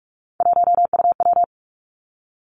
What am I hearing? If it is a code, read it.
Morse code, heard as a keyed tone, about 46 wpm, 700 Hz. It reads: 1UW
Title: Three Challenging Letter-Number Combinations 45wpm